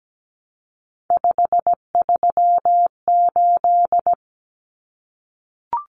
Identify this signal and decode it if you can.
Morse code, heard as a keyed tone, about 17 wpm, 700 Hz. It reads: 538
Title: Common Cut Numbers - 3-Digit 17wpm